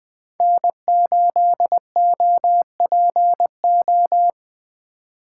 Morse code, heard as a keyed tone, about 20 wpm, 700 Hz. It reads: N8OPO
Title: Speed Racing - USA 1x3 Call Signs 17wpm